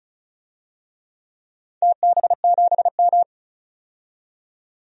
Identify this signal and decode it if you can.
Morse code, heard as a keyed tone, about 36 wpm, 700 Hz. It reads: TB7M
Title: DXCC-100 Call Signs - 4 Characters - 35wpm